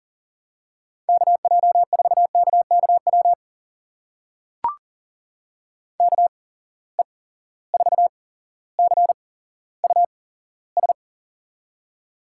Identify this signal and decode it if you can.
Morse code, heard as a keyed tone, about 40 wpm, spaced as for 17 wpm, 700 Hz. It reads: KJ4KKW KE4CUS
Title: USA Call Signs - 6 Characters - 8x Character Spacing - 40wpm